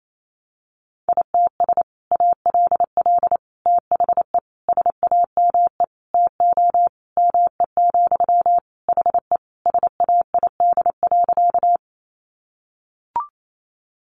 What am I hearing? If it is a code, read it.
Morse code, heard as a keyed tone, about 28 wpm, 700 Hz. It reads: ITS ALL THE SAME TO ME, HE SAID.
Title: Sentences from Top 600 Words 28wpm